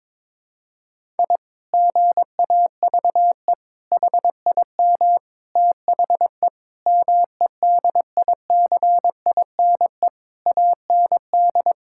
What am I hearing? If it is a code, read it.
Morse code, heard as a keyed tone, about 22 wpm, 700 Hz. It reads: I GAVE HIM THE MEDICINE AND
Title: Sentences from Top 1000 Words - Review Part 6 - 22wpm